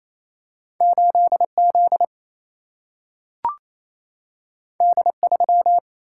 Morse code, heard as a keyed tone, about 28 wpm, 700 Hz. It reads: 8Z D3
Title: Single-Letter Single-Number 28wpm